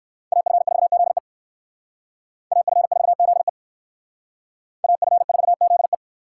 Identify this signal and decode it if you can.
Morse code, heard as a keyed tone, about 57 wpm, 700 Hz. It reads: AF4ZE AF4ZE AF4ZE